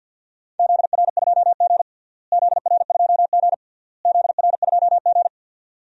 Morse code, heard as a keyed tone, about 46 wpm, 700 Hz. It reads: ZR2G ZR2G ZR2G
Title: DXCC-100 Call Signs - 4 Characters - Repeated 3x - 50wpm